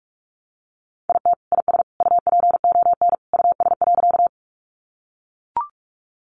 Morse code, heard as a keyed tone, about 44 wpm, 700 Hz. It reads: IT IS UPON US.
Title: Sentences from Top 1100 Words 45wpm